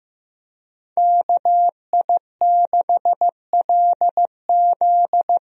Morse code, heard as a keyed tone, about 15 wpm, 700 Hz. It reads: KI6LZ
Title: USA Call Signs - 5 Characters - Repeated 3x - 15wpm